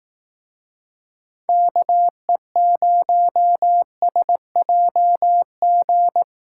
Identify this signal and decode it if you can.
Morse code, heard as a keyed tone, about 18 wpm, 700 Hz. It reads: KE0SJG
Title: Speed Racing - USA Call Signs 15wpm